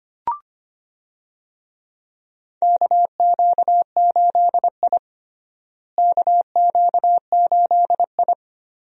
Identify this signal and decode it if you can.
Morse code, heard as a keyed tone, about 25 wpm, 700 Hz. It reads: KQ8I KQ8I
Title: USA Call Signs - 4 Characters - Repeated 3x - 25wpm